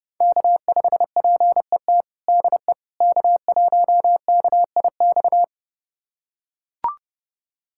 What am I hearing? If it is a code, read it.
Morse code, heard as a keyed tone, about 30 wpm, 700 Hz. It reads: K5PET DE K1KIX
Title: USA Callsign DE Callsign - 5 Characters - 30wpm